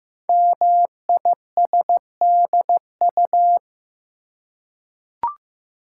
Morse code, heard as a keyed tone, about 15 wpm, 700 Hz. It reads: MISDU
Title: Code Groups - 5 Letters - 15wpm